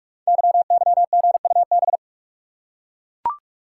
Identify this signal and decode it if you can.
Morse code, heard as a keyed tone, about 44 wpm, 690 Hz. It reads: YYGUB
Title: Code Groups - 5 Letters - 45wpm